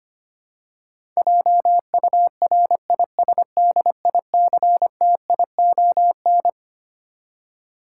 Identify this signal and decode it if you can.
Morse code, heard as a keyed tone, about 25 wpm, 700 Hz. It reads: JURISDICTION
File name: Twelve Letter Words - 25wpm